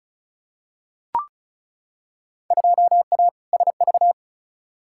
Morse code, heard as a keyed tone, about 36 wpm, 700 Hz. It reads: 2A SV
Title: Field Day Exchanges - 35wpm